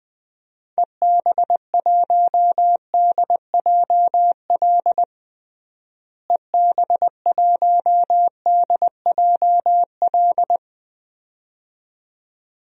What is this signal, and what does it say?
Morse code, heard as a keyed tone, about 20 wpm, 700 Hz. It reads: EB1DJL EB1DJL
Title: DXCC-100 Call Signs - 6 Characters - Repeated 2x - 20wpm